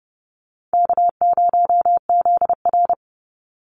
Morse code, heard as a keyed tone, about 30 wpm, 700 Hz. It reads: K0ZR
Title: Easy Call Signs - 30wpm